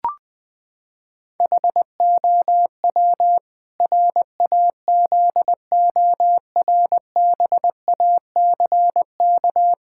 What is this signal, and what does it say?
Morse code, heard as a keyed tone, about 20 wpm, 700 Hz. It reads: HOW RAZORBACK
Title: Pangram Challenge v2 - 20wpm